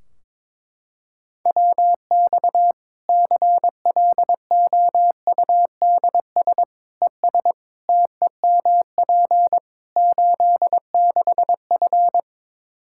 Morse code, heard as a keyed tone, about 21 wpm, 700 Hz. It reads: WX CLOUDS ES TEMP 86F
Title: Practice QSO - Exchange 2 - Info - 22wpm